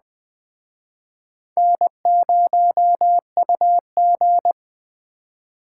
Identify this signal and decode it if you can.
Morse code, heard as a keyed tone, about 19 wpm, 700 Hz. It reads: N0UG